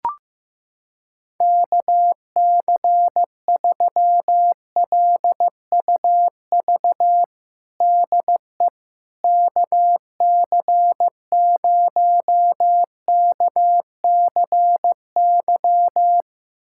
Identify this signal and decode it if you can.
Morse code, heard as a keyed tone, about 15 wpm, 700 Hz. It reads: KC3LUV DE KC0KCY
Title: USA Callsign DE Callsign 15wpm